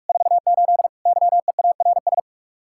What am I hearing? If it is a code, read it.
Morse code, heard as a keyed tone, about 47 wpm, 700 Hz. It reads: 48 YEARS